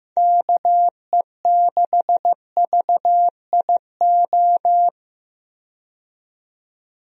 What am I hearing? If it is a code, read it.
Morse code, heard as a keyed tone, about 15 wpm, 700 Hz. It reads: KE6VIO
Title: USA Call Signs - 6 Characters - 15wpm